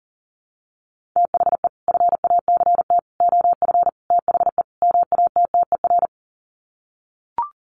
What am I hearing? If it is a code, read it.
Morse code, heard as a keyed tone, about 40 wpm, 700 Hz. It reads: THE FACT OF THE MATTER
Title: Sets of 5 Words 40wpm